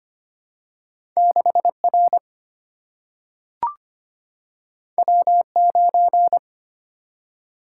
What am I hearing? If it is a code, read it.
Morse code, heard as a keyed tone, about 25 wpm, 700 Hz. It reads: 6R W9
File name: Two Letter-Number Combinations - 25wpm